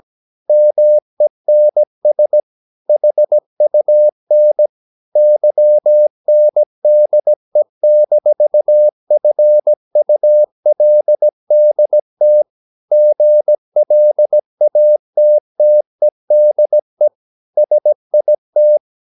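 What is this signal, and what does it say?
Morse code, heard as a keyed tone, about 17 wpm, 600 Hz. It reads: MENS HUN YNDE-FULDT GLATTEDE SIT
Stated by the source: Slow (QRS) Morse practice at 17 wpm, from CW med Gnister.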